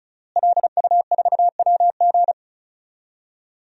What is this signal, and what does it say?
Morse code, heard as a keyed tone, about 34 wpm, 700 Hz. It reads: LU4WG
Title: DXCC-100 Call Signs - 5 Characters - Repeated 3x - 35wpm